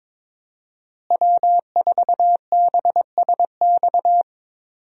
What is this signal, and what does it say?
Morse code, heard as a keyed tone, about 22 wpm, 700 Hz. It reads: W4BSX
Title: USA Call Signs - 5 Characters - Repeated 2x - 22wpm